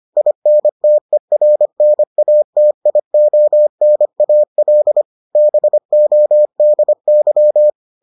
Morse code, heard as a keyed tone, about 25 wpm, 600 Hz. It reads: INTERNATIONAL BODY